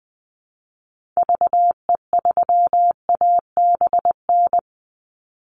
Morse code, heard as a keyed tone, about 20 wpm, 700 Hz. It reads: VE3ABN
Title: Call Signs Repeated 2x - 20wpm